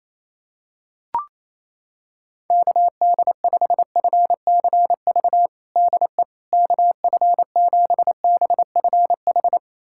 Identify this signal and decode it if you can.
Morse code, heard as a keyed tone, about 28 wpm, 700 Hz. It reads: KD5FCV DE KF7BFH
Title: USA Callsign DE Callsign 28wpm